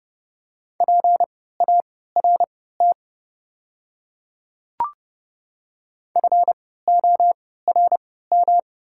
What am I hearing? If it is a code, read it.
Morse code, heard as a keyed tone, about 31 wpm, spaced as for 17 wpm, 690 Hz. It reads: PART FORM
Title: 3x Character Spacing - Top 200 Words - 30wpm